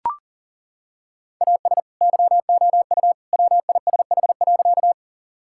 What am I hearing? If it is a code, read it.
Morse code, heard as a keyed tone, about 40 wpm, 700 Hz. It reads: AS YOU WISH.
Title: Sentences from Top 1600 Words 40wpm